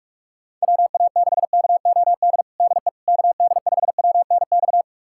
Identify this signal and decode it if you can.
Morse code, heard as a keyed tone, about 41 wpm, 700 Hz. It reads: WA6KOD DE KD5WNX
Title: USA Callsign DE Callsign - 6 Characters - 45wpm